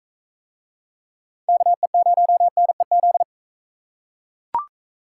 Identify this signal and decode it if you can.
Morse code, heard as a keyed tone, about 37 wpm, 700 Hz. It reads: KE0NEZ